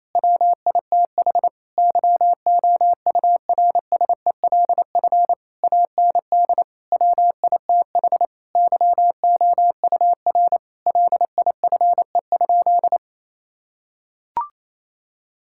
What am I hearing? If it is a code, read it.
Morse code, heard as a keyed tone, about 28 wpm, 700 Hz. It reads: WITH YOURSELF AND WITH YOUR LIFE?
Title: Sentences from Top 1000 Words - Review Part 6 - 28wpm